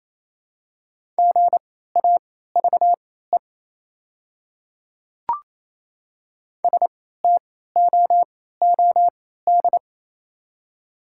Morse code, heard as a keyed tone, about 28 wpm, spaced as for 16 wpm, 690 Hz. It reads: GAVE STOOD
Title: 3x Character Spacing - Top 500 Words - 28wpm